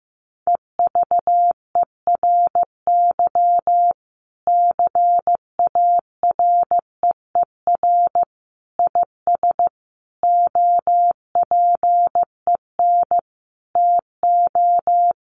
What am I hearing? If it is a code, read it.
Morse code, heard as a keyed tone, about 15 wpm, 700 Hz. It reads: EVERY CAREER IS OPEN TO
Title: Sentences from Top 1000 Words 15wpm